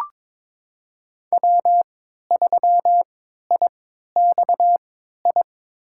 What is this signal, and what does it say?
Morse code, heard as a keyed tone, about 22 wpm, spaced as for 13 wpm, 700 Hz. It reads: W3IXI
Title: USA Call Signs - 3x Character Spacing - 22wpm